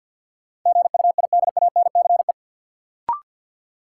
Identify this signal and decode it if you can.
Morse code, heard as a keyed tone, about 53 wpm, 700 Hz. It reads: GUIDANCE
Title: Sentences from Top 1600 Words - New Word Drill - 50wpm